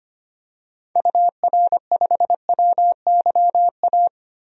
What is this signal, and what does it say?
Morse code, heard as a keyed tone, about 25 wpm, 700 Hz. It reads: UR5WYA
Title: DXCC-100 Call Signs - 6 Characters - Repeated 2x - 25wpm